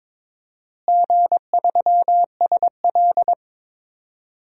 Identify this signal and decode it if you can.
Morse code, heard as a keyed tone, about 22 wpm, 700 Hz. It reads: G3SL
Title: DXCC-100 Call Signs - 4 Characters - Repeated 2x - 22wpm